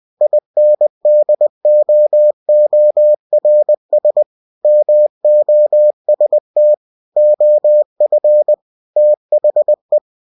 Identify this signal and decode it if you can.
Morse code, heard as a keyed tone, about 20 wpm, 600 Hz. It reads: INDOORS MOST OF THE
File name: cw-books / winnie-the-pooh / winnie-20wpm-page-0095